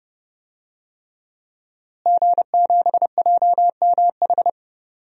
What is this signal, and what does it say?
Morse code, heard as a keyed tone, about 30 wpm, 700 Hz. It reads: G7JMH
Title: DXCC-100 Call Signs - 5 Characters - Repeated 3x - 30wpm